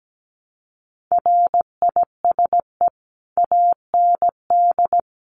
Morse code, heard as a keyed tone, about 17 wpm, 700 Hz. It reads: RISE AND
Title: Sentences from Top 900 Words 17wpm